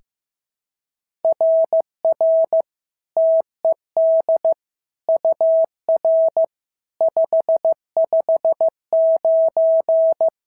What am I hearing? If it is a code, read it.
Morse code, heard as a keyed tone, about 14 wpm, 650 Hz. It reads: RR TED UR 559
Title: SOTA QSO Practice 15wpm